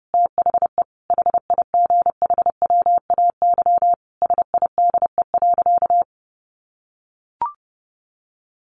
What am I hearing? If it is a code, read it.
Morse code, heard as a keyed tone, about 30 wpm, 700 Hz. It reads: THE HIGHWAY SIDE.
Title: Sentences from Top 1400 Words 30wpm